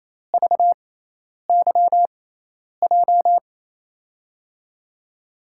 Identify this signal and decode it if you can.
Morse code, heard as a keyed tone, about 28 wpm, spaced as for 10 wpm, 700 Hz. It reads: VYJ
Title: USA Call Signs - 6 Characters - 6x Character Spacing - 28wpm